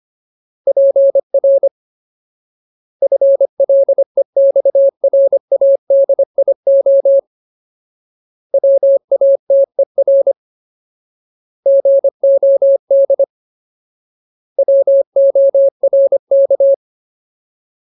Morse code, heard as a keyed tone, about 25 wpm, 550 Hz. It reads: PR FLEXRADIO WATER GOD WORK
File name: Ultimate Word List / 25WPM Ultimate